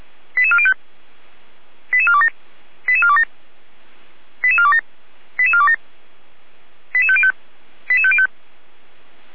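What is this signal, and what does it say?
Five-tone selective call (ZVEI), heard as quick runs of five tones, tones of about 70 ms. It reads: 88405 88427 88427 88427 88427 88505 88505
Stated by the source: ZVEI five-tone selective call 88405 received by SDR.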